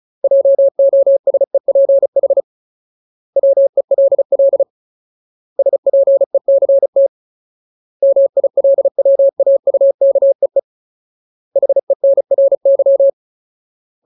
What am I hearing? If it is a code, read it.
Morse code, heard as a keyed tone, about 34 wpm, 550 Hz. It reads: JOSEPH WELL SPECT MILWAUKEE HENRY
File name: Ultimate Word List / 35WPM Ultimate